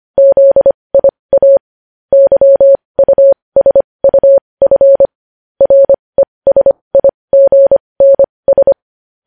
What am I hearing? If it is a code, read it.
Morse code, heard as a keyed tone, about 25 wpm, spaced as for 20 wpm, 550 Hz. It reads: ZIA YUSUF RESIGNS
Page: news-headlines-in-morse-code-at-20-wpm